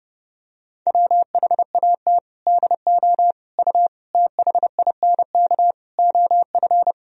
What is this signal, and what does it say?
Morse code, heard as a keyed tone, about 32 wpm, 700 Hz. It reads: WHAT DO U THINK OF
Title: Sentences from Top 200 Words and Abbreviations 30wpm